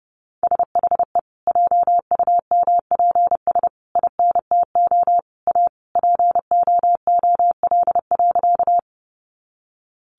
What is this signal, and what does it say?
Morse code, heard as a keyed tone, about 30 wpm, 700 Hz. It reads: SHE JUMPS INTO A POOL.
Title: Additional Easy Phrases 30wpm